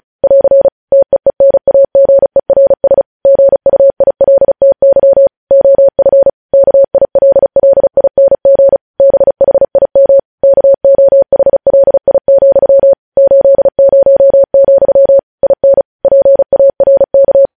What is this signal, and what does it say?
Morse code, heard as a keyed tone, about 36 wpm, 550 Hz. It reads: <AR> TEENAGERS GUILTY OF KILLING BHIM KOHLI, 80, IN PARK
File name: podcast / News headlines-1x-35x35